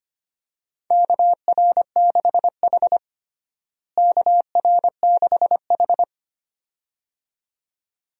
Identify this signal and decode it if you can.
Morse code, heard as a keyed tone, about 25 wpm, 700 Hz. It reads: KR6H KR6H
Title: USA Call Signs - 4 Characters - Repeated 2x - 25wpm